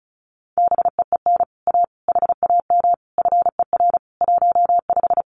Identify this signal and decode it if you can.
Morse code, heard as a keyed tone, about 36 wpm, 700 Hz. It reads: BEEN A HAM FER 15
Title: Common QSO Phrases Encore 35wpm